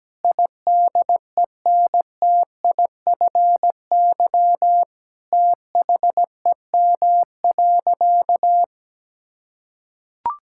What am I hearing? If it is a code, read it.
Morse code, heard as a keyed tone, about 17 wpm, 700 Hz. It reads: IDENTIFY THEM.
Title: Sentences from Top 1000 Words - Review Part 6 - 17wpm